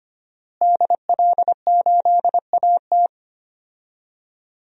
Morse code, heard as a keyed tone, about 25 wpm, 700 Hz. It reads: DL8AT
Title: DXCC-100 Call Signs - 5 Characters - Repeated 3x - 25wpm